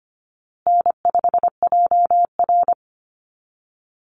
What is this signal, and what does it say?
Morse code, heard as a keyed tone, about 25 wpm, 700 Hz. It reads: N5JR
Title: Easy Call Signs Repeated 2x - 25wpm